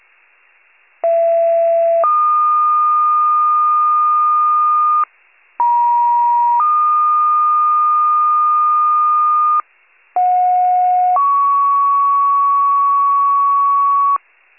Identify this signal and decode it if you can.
Two-tone page, heard as a one-second tone, then a three-second tone, about 670 then 1220 Hz, about 950 then 1220 Hz, about 730 then 1120 Hz. It is heard from 1.0 to 14.2 s.